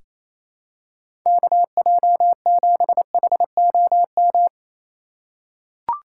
Morse code, heard as a keyed tone, about 27 wpm, 700 Hz. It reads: KJ7HOM
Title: USA Call Signs - 2x Character Spacing - 28wpm